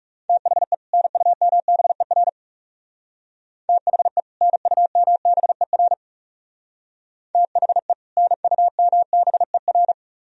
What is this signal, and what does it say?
Morse code, heard as a keyed tone, about 37 wpm, 700 Hz. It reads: THE NUMBER THE NUMBER THE NUMBER
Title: Speed-Racing - Sets of 2 Words 30wpm